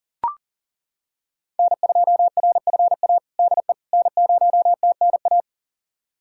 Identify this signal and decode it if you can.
Morse code, heard as a keyed tone, about 37 wpm, 700 Hz. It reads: N2RFA DE N0TNA
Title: USA Callsign DE Callsign - 5 Characters - 40wpm